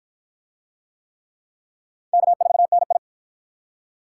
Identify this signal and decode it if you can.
Morse code, heard as a keyed tone, about 53 wpm, 700 Hz. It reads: K4NI